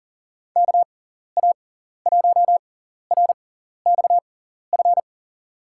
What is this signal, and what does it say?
Morse code, heard as a keyed tone, about 39 wpm, spaced as for 14 wpm, 700 Hz. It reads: KA1RXF
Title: USA Call Signs - 6 Characters - 6x Character Spacing - 40wpm